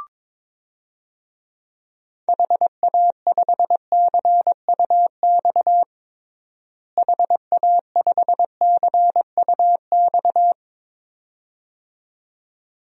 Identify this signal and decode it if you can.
Morse code, heard as a keyed tone, about 22 wpm, 700 Hz. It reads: HA5CUX HA5CUX
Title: DXCC-100 Call Signs - 6 Characters - Repeated 2x - 22wpm